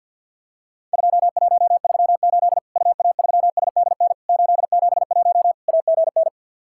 Morse code, heard as a keyed tone, about 50 wpm, 700 Hz. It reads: 2138 UA3SDN 871 AGN
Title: Worked All Europe DX Contest - QTC Exchange with Repeats - 50wpm